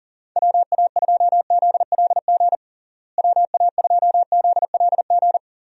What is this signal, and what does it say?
Morse code, heard as a keyed tone, about 40 wpm, 700 Hz. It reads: WA2ZLG WA2ZLG
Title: USA Call Signs - 6 Characters - Repeated 3x - 40wpm